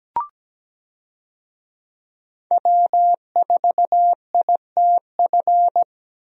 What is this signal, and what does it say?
Morse code, heard as a keyed tone, about 17 wpm, 700 Hz. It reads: W4ITF
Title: USA Call Signs - 5 Characters - Repeated 3x - 17wpm